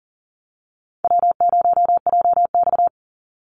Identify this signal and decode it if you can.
Morse code, heard as a keyed tone, about 40 wpm, 700 Hz. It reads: W0JX
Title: Call Signs - 40wpm